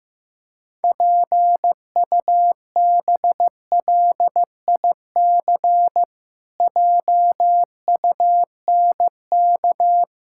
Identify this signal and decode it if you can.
Morse code, heard as a keyed tone, about 15 wpm, 700 Hz. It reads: PUBLIC JUNK
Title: Pangram Challenge v2 - 15wpm